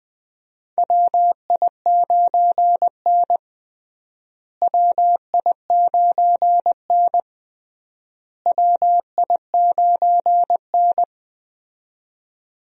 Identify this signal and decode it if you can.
Morse code, heard as a keyed tone, about 20 wpm, 700 Hz. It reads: WI9N WI9N WI9N
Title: USA Call Signs - 4 Characters - Repeated 3x - 20wpm